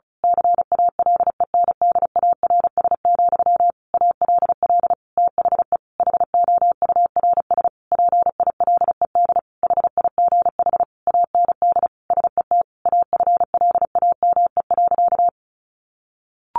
Morse code, heard as a keyed tone, about 34 wpm, 700 Hz. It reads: CALENDARS, ALL THE HOURS PILED HIGH AND SET AFLAME.
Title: A Sound Of Thunder 35wpm